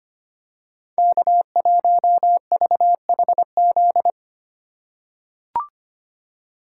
Morse code, heard as a keyed tone, about 25 wpm, 700 Hz. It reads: K1VHZ
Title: USA Call Signs - 5 Characters - 3x Character Spacing - 25wpm